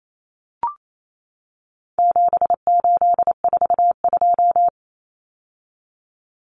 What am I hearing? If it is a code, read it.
Morse code, heard as a keyed tone, about 28 wpm, 700 Hz. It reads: 7842
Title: Numbers - 4-Digit 28wpm